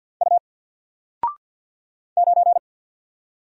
Morse code, heard as a keyed tone, about 48 wpm, 700 Hz. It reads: U 9
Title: Single Character (Letter:Number:Punctuation:Prosign) 50wpm